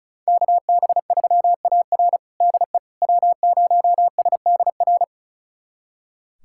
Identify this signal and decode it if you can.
Morse code, heard as a keyed tone, about 36 wpm, 700 Hz. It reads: KB3AR DE W0SDR
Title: USA Callsign DE Callsign - 5 Characters - 35wpm